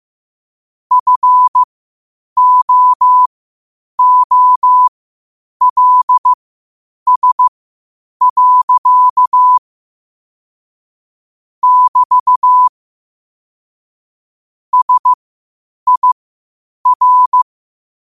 Morse code, heard as a keyed tone, about 15 wpm, spaced as for 8 wpm, 1000 Hz. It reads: FOOLS. <BT> SIR
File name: qotdfiles / 8wpm